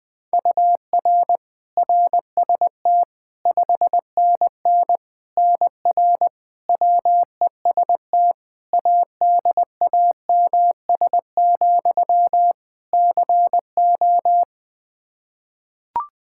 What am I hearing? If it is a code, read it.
Morse code, heard as a keyed tone, about 20 wpm, 700 Hz. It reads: UR RST 5NN NR WEST ADAMS, CO
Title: Practice QSO - Exchange 1 - Beginning - 20wpm